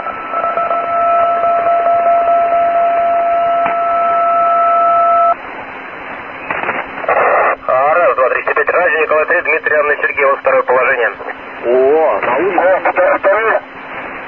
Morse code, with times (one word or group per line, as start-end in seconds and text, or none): none